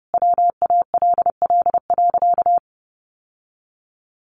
Morse code, heard as a keyed tone, about 30 wpm, 700 Hz. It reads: WALL.
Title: Sentences from Top 1100 Words 30wpm